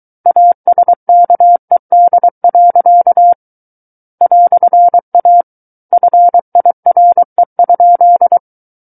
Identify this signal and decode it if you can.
Morse code, heard as a keyed tone, about 23 wpm, 700 Hz. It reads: ASKED. "A FIRE?
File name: Book One / 14-Chapter 14 - 23 WPM